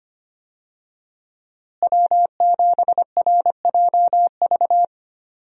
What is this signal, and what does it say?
Morse code, heard as a keyed tone, about 25 wpm, 700 Hz. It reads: W7RJV
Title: USA Call Signs - 5 Characters - Repeated 3x - 25wpm